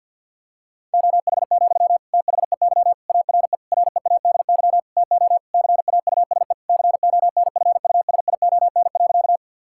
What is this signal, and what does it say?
Morse code, heard as a keyed tone, about 46 wpm, 700 Hz. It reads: OH, THEY ARE READY TO CAUSE CONFUSION.
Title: Sentences from Top 1600 Words 50wpm